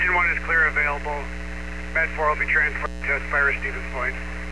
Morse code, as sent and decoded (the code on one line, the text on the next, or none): none